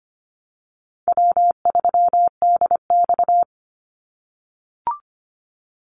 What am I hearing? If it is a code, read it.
Morse code, heard as a keyed tone, about 25 wpm, 690 Hz. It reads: W3DX
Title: NA SOTA Activator Callsigns 25wpm